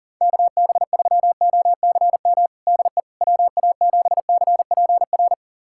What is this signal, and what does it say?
Morse code, heard as a keyed tone, about 40 wpm, 700 Hz. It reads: KB3OCM DE WA7CPR
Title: USA Callsign DE Callsign 40wpm